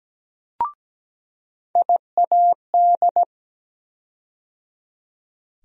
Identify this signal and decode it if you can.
Morse code, heard as a keyed tone, about 17 wpm, 700 Hz. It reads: IAD
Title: USA Call Sign Postfixes - 3 Letters 17wpm